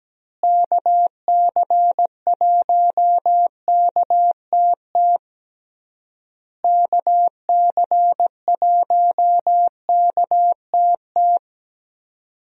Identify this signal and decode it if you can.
Morse code, heard as a keyed tone, about 17 wpm, 700 Hz. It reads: KC1KTT KC1KTT
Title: USA Call Signs - 6 Characters - Repeated 3x - 17wpm